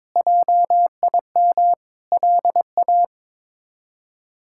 Morse code, heard as a keyed tone, about 22 wpm, 700 Hz. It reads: JIM LA
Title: K1USN SST QSO Practice - 22wpm